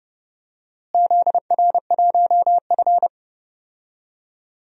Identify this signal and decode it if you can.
Morse code, heard as a keyed tone, about 30 wpm, 700 Hz. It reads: ZR1F
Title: DXCC-100 Call Signs - 4 Characters - Repeated 2x - 30wpm